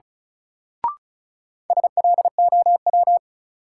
Morse code, heard as a keyed tone, about 36 wpm, 700 Hz. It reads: SLOW
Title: Sentences from Top 400 Words - New Word Drill - 35wpm